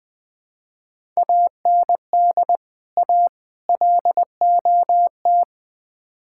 Morse code, heard as a keyed tone, about 21 wpm, 700 Hz. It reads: AND A LOT
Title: Build A Phrase - 4-Word Phrases - 20wpm